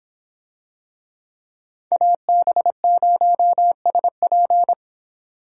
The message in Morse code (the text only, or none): AB0SP